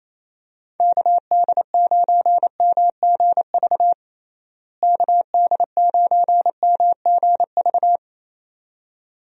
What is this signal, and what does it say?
Morse code, heard as a keyed tone, about 27 wpm, 700 Hz. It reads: KD9MGV KD9MGV
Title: USA Call Signs - 6 Characters - Repeated 2x - 28wpm